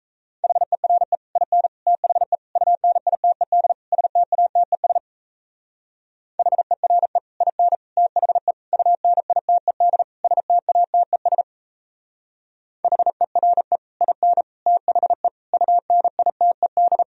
Morse code, heard as a keyed tone, about 34 wpm, 700 Hz. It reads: HERE IN THE UNITED STATES HERE IN THE UNITED STATES HERE IN THE UNITED
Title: Speed-Racing - Sets of 5 Words 28wpm